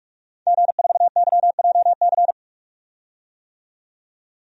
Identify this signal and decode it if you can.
Morse code, heard as a keyed tone, about 42 wpm, 700 Hz. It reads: G4YJC
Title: DXCC-100 Call Signs - 5 Characters - 45wpm